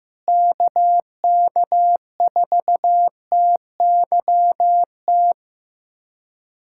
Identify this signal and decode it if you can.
Morse code, heard as a keyed tone, about 15 wpm, 700 Hz. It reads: KK4TYT
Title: USA Callsign DE Callsign - 6 Characters - 15wpm